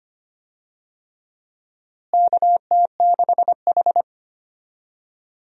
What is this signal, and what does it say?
Morse code, heard as a keyed tone, about 25 wpm, 700 Hz. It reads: KT6H
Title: USA Call Signs - 4 Characters - Repeated 3x - 25wpm